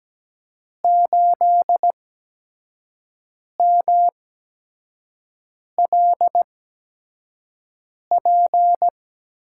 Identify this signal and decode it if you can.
Morse code, heard as a keyed tone, about 17 wpm, spaced as for 5 wpm, 700 Hz. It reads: 8MLP
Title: USA Call Signs - 5 Characters - 8x Character Spacing - 17wpm